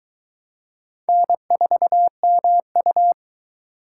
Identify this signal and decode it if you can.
Morse code, heard as a keyed tone, about 22 wpm, 700 Hz. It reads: N4MU